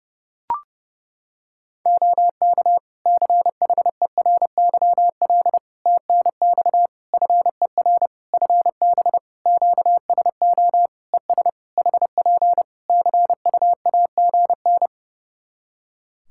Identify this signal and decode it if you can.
Morse code, heard as a keyed tone, about 30 wpm, 700 Hz. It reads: OK CHERYL TNX FER FB QSO ES HP CUAGN
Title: Practice QSO - Exchange 4 - Ending - 30wpm